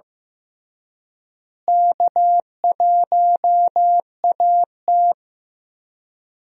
Morse code, heard as a keyed tone, about 15 wpm, 700 Hz. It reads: K1AT
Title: USA Call Signs - 4 Characters - Repeated 3x - 15wpm